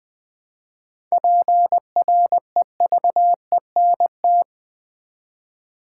Morse code, heard as a keyed tone, about 20 wpm, 700 Hz. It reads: PREVENT
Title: Sentences from Top 1100 Words - New Word Drill Alt v2 - 20wpm